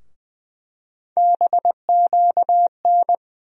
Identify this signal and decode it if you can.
Morse code, heard as a keyed tone, about 20 wpm, 700 Hz. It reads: BQN
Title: USA Call Sign Postfixes - 3 Letters 20wpm